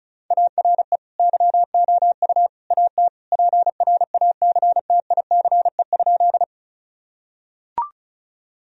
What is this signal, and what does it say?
Morse code, heard as a keyed tone, about 36 wpm, 700 Hz. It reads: ARE YOU AT PRACTICE?
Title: Sentences from Top 1000 Words - Review Part 4 - 35wpm